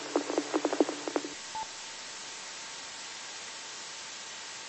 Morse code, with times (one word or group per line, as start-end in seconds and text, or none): none